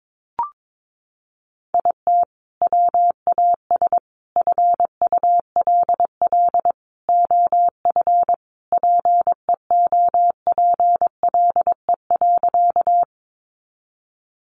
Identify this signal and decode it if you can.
Morse code, heard as a keyed tone, about 22 wpm, 700 Hz. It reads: IT WAS FULL OF PEOPLE.
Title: Sentences from Top 500 Words 22wpm